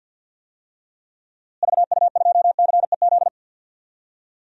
Morse code, heard as a keyed tone, about 47 wpm, 700 Hz. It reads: VU2CEZ
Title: DXCC-100 Call Signs - 6 Characters - Repeated 2x - 50wpm